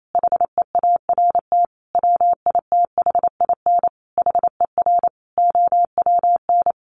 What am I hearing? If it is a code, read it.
Morse code, heard as a keyed tone, about 28 wpm, 700 Hz. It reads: HEART WITHIN HER OWN
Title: Sentences from Top 1200 Words 28wpm